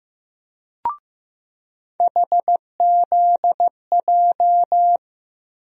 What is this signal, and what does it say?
Morse code, heard as a keyed tone, about 15 wpm, 700 Hz. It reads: HZJ
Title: USA Call Sign Postfixes - 3 Letters 15wpm